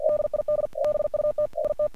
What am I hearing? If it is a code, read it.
Morse code, heard as a keyed tone, about 48 wpm, 1200 Hz, with noise. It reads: DID BUT IT
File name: archived / CRSnet